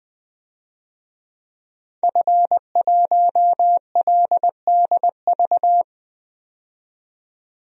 Morse code, heard as a keyed tone, about 20 wpm, 700 Hz. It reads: F1LDV
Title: DXCC-100 Call Signs - 5 Characters - 20wpm